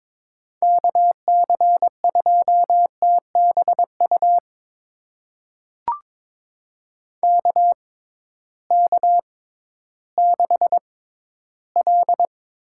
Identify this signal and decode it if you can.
Morse code, heard as a keyed tone, about 22 wpm, spaced as for 13 wpm, 700 Hz. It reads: KC2TBU KK6L
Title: USA Call Signs - 6 Characters - 6x Character Spacing - 22wpm